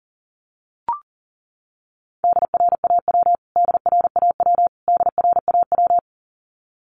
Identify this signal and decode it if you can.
Morse code, heard as a keyed tone, about 40 wpm, 700 Hz. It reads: DRAW DRAW DRAW
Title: Top 500 Words Repeated 40wpm